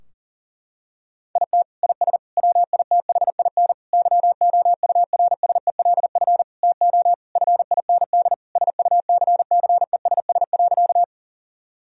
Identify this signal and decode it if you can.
Morse code, heard as a keyed tone, about 40 wpm, 700 Hz. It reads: IT IS WITHIN YOURSELF TO FIND SUCCESS.
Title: Sentences from Top 1000 Words - Review Part 4 - 40wpm